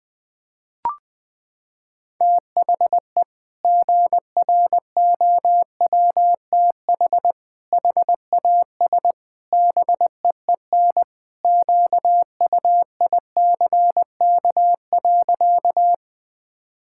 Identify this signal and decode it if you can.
Morse code, heard as a keyed tone, about 20 wpm, 700 Hz. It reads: THE GROWTH HAS BEEN QUICK.
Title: Sentences from Top 1000 Words - Review Part 5 - 20wpm